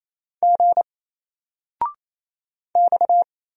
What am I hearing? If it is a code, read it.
Morse code, heard as a keyed tone, about 27 wpm, 700 Hz. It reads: G X
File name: Single-Letter-Number-28wpm